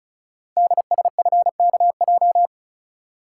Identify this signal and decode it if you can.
Morse code, heard as a keyed tone, about 36 wpm, 690 Hz. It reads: DSFKJ
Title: Code Groups - 5 Letters - 35wpm